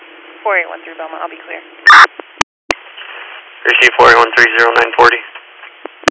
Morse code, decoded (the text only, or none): none